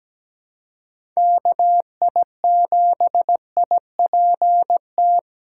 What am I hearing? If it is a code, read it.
Morse code, heard as a keyed tone, about 17 wpm, 700 Hz. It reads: KI7IPT
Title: USA Call Signs - 6 Characters - 3x Character Spacing - 17wpm